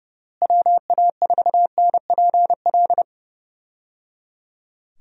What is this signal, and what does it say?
Morse code, heard as a keyed tone, about 30 wpm, 700 Hz. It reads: WA4NPL
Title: USA Call Signs - 6 Characters - Repeated 3x - 30wpm